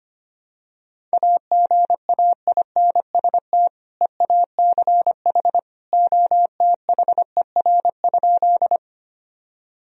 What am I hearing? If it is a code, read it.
Morse code, heard as a keyed tone, about 25 wpm, 700 Hz. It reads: AGAINST EACH OTHER?
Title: Sentences from Top 1000 Words - Review Part 1 - 25wpm